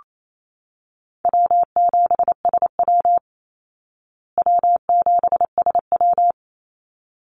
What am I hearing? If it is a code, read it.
Morse code, heard as a keyed tone, about 28 wpm, 700 Hz. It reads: W7SW W7SW
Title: Easy Call Signs Repeated 2x - 28wpm